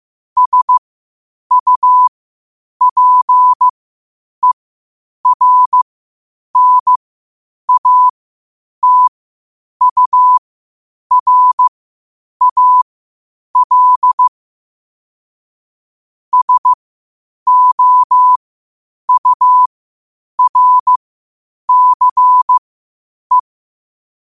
Morse code, heard as a keyed tone, about 15 wpm, spaced as for 8 wpm, 1000 Hz. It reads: SUPERNATURAL SOURCE
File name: qotdfiles / 8wpm